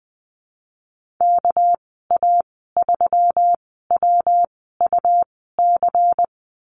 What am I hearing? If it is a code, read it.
Morse code, heard as a keyed tone, about 20 wpm, spaced as for 14 wpm, 700 Hz. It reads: KA3WUC
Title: 219 Lesson 35g - Number 0 - Call Signs 20wpm